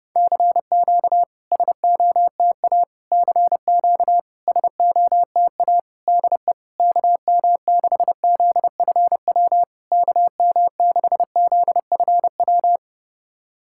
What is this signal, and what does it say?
Morse code, heard as a keyed tone, about 30 wpm, 700 Hz. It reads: CQ SOTA CQ SOTA DE KM6ZFW KM6ZFW
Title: SOTA QSO Practice 30wpm